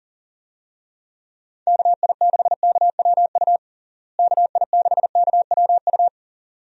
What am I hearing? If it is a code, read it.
Morse code, heard as a keyed tone, about 40 wpm, 700 Hz. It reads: KI6KWU KI6KWU
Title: USA Call Signs - 6 Characters - Repeated 2x - 40wpm